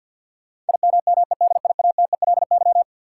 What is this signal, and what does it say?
Morse code, heard as a keyed tone, about 52 wpm, 700 Hz. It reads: IMMEDIATELY
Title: Sentences from Top 1100 Words - New Word Drill Alt - 50wpm